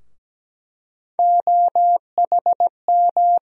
Morse code, heard as a keyed tone, about 17 wpm, 700 Hz. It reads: OHM
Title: USA Call Sign Postfixes - 3 Letters 17wpm